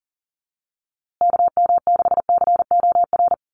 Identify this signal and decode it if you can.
Morse code, heard as a keyed tone, about 40 wpm, 700 Hz. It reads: KM6COR
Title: USA Call Signs Repeated 3x 40wpm